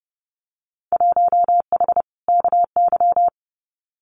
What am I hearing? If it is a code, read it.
Morse code, heard as a keyed tone, about 30 wpm, 700 Hz. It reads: 1H KY
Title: Winter Field Day Exchanges 30wpm